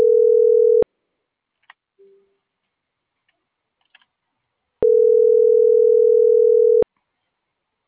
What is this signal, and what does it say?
Ringback tone (North America), 440 + 480 Hz.